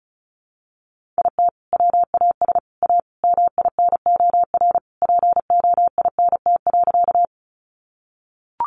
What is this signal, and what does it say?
Morse code, heard as a keyed tone, about 34 wpm, 700 Hz. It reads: IT WAS A MINOR POINT.
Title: Sentences from Top 1500 Words 35wpm